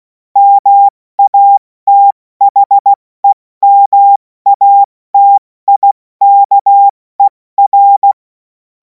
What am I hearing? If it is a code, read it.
Morse code, heard as a keyed tone, about 16 wpm, spaced as for 14 wpm, 800 Hz. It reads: MATHEMATIKER
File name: Morsekurs / Witze / Witz 14. April 2026 70 F80